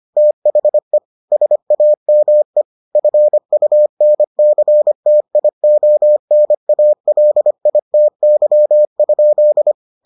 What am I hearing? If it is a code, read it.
Morse code, heard as a keyed tone, about 25 wpm, 600 Hz. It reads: THE SAME FUNCTIONALITY?